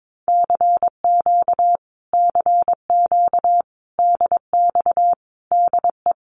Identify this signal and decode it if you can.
Morse code, heard as a keyed tone, about 22 wpm, 700 Hz. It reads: CQ CQ DX DE
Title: Common QSO Phrases Encore 22wpm